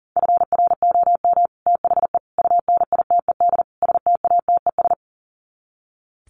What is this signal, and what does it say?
Morse code, heard as a keyed tone, about 40 wpm, 700 Hz. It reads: FROM THE UNITED STATES
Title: Sets of 4 Words 40wpm